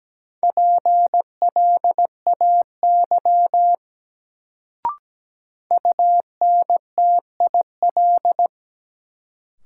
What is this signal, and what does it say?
Morse code, heard as a keyed tone, about 17 wpm, 700 Hz. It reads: PLAY UNTIL
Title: Sets of 5 Words Encore - New Word Drill - 17wpm